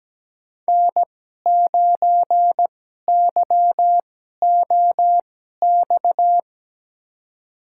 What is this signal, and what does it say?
Morse code, heard as a keyed tone, about 17 wpm, spaced as for 12 wpm, 700 Hz. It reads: N9YOX
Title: USA Call Signs - 5 Characters - 2x Character Spacing - 17wpm